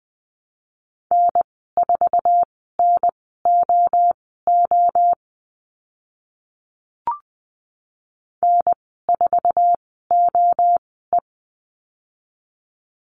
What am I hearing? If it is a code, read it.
Morse code, heard as a keyed tone, about 20 wpm, spaced as for 15 wpm, 700 Hz. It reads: N4NOO N4OE
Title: 069 Lesson 12g - Letter d - Call Signs 20wpm